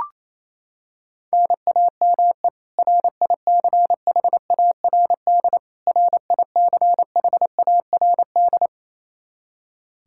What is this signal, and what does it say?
Morse code, heard as a keyed tone, about 27 wpm, 700 Hz. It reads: NAME RICHARD RICHARD
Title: Practice QSO - Exchange 1 - Beginning - 28wpm